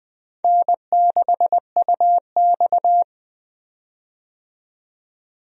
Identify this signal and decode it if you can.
Morse code, heard as a keyed tone, about 21 wpm, 700 Hz. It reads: N6UX
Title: USA Call Signs - 4 Characters - 20wpm